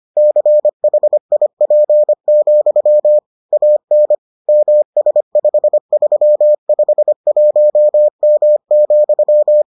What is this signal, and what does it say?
Morse code, heard as a keyed tone, about 25 wpm, 600 Hz. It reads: CHIP, AN MS5351M,